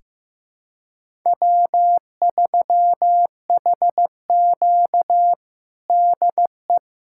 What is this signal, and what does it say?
Morse code, heard as a keyed tone, about 16 wpm, 700 Hz. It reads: W3HQ DE
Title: USA Callsign DE Callsign - 4 Characters - 15wpm